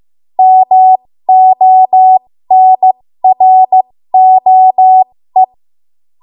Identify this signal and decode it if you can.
Morse code, heard as a keyed tone, about 15 wpm, spaced as for 13 wpm, 750 Hz. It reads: MONROE
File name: Morse / 13WPM